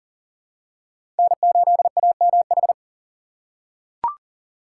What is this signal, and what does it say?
Morse code, heard as a keyed tone, about 40 wpm, 700 Hz. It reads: N8AMH
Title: USA Call Signs - 5 Characters - 3x Character Spacing - 40wpm